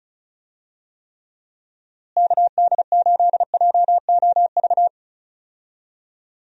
Morse code, heard as a keyed tone, about 36 wpm, 700 Hz. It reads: KD8JOV
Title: USA Call Signs - 6 Characters - Repeated 3x - 35wpm